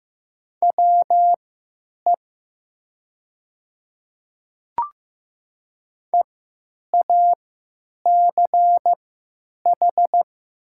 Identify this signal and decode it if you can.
Morse code, heard as a keyed tone, about 15 wpm, spaced as for 9 wpm, 690 Hz. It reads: WE EACH
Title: 3x Character Spacing - Top 100 Words - 15wpm